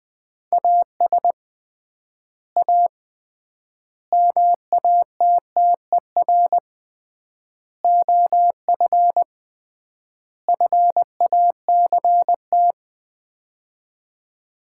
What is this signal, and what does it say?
Morse code, heard as a keyed tone, about 20 wpm, 700 Hz. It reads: AS A MATTER OF FACT
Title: 3x Word-Spacing - Sets of 5 Words - 20wpm